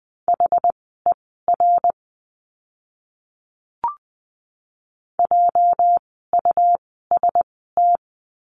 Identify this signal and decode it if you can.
Morse code, heard as a keyed tone, about 20 wpm, spaced as for 14 wpm, 700 Hz. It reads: HER JUST
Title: 230 Lesson 37f - Letter x - Words 20wpm